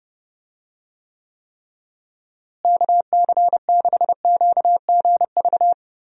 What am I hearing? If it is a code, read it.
Morse code, heard as a keyed tone, about 30 wpm, 700 Hz. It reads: KC6QGV
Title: USA Call Signs - 6 Characters - Repeated 3x - 30wpm